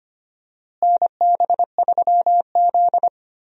Morse code, heard as a keyed tone, about 25 wpm, 700 Hz. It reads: NB3Z
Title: USA Call Signs - 4 Characters - Repeated 3x - 25wpm